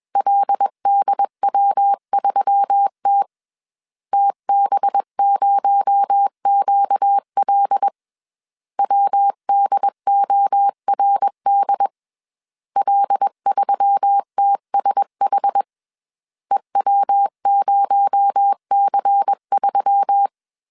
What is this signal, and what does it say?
Morse code, heard as a keyed tone, about 21 wpm, 800 Hz. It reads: LDW3T TB0QL WDORD L3TSH EW0C3
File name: CW / DET / 1 6 1500C S A 20wpm 2048